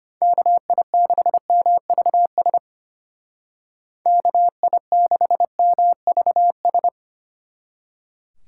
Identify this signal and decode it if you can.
Morse code, heard as a keyed tone, about 31 wpm, 700 Hz. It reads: KI6MVS KI6MVS
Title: Speed Racing - USA 2x3 Call Signs 25wpm